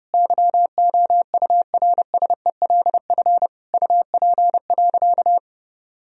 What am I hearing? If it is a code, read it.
Morse code, heard as a keyed tone, about 32 wpm, 700 Hz. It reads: YOURSELF UP.
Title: Sentences from Top 1000 Words - Review Part 1 - 30wpm